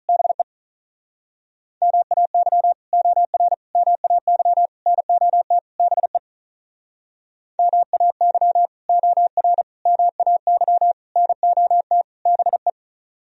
Morse code, heard as a keyed tone, about 44 wpm, 700 Hz. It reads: BE MAY OR MAY NOT BE MAY OR MAY NOT BE